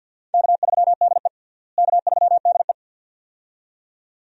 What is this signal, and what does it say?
Morse code, heard as a keyed tone, about 52 wpm, 700 Hz. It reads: K3DE K3DE
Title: USA Call Signs - 4 Characters - Repeated 2x - 50wpm